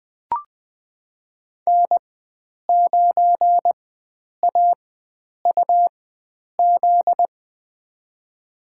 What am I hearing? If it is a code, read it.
Morse code, heard as a keyed tone, about 20 wpm, spaced as for 9 wpm, 700 Hz. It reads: N9AUZ
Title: USA Call Signs - 5 Characters - 4x Character Spacing - 20wpm